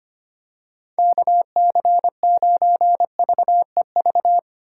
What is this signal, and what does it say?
Morse code, heard as a keyed tone, about 25 wpm, 700 Hz. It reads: KC9VEV
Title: USA Call Signs - 6 Characters - 2x Character Spacing - 25wpm